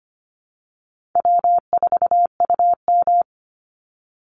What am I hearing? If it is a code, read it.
Morse code, heard as a keyed tone, about 25 wpm, 700 Hz. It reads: W4UM
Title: Easy Call Signs Repeated 3x - 25wpm